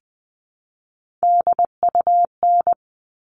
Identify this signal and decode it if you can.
Morse code, heard as a keyed tone, about 20 wpm, 700 Hz. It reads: DUN
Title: All Cut Numbers - 3-Digit 20wpm